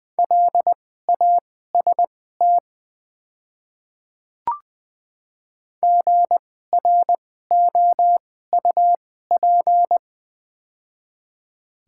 Morse code, heard as a keyed tone, about 20 wpm, spaced as for 14 wpm, 700 Hz. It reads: LAST GROUP
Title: 2x Character Spacing - Top 300 Words - 20wpm